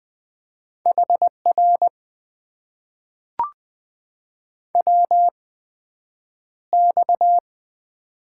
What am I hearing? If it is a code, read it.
Morse code, heard as a keyed tone, about 20 wpm, spaced as for 9 wpm, 700 Hz. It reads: HR WX
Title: Top 90 QSO Elements - 8x Character Spacing - 20wpm